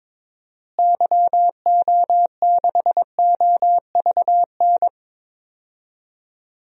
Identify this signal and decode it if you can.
Morse code, heard as a keyed tone, about 21 wpm, 700 Hz. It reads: YO6OVN
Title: DXCC-100 Call Signs - 6 Characters - Repeated 2x - 22wpm